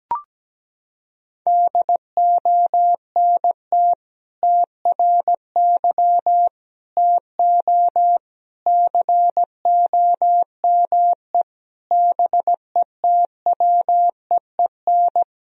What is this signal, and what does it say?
Morse code, heard as a keyed tone, about 17 wpm, 700 Hz. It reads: DONT TRY TO COME BETWEEN
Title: Sentences from Top 1000 Words - Review Part 1 - 17wpm